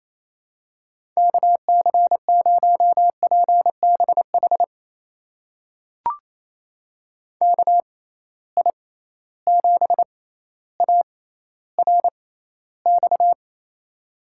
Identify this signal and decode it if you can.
Morse code, heard as a keyed tone, about 28 wpm, spaced as for 14 wpm, 700 Hz. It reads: KC0PBH KI7ARX
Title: USA Call Signs - 6 Characters - 6x Character Spacing - 28wpm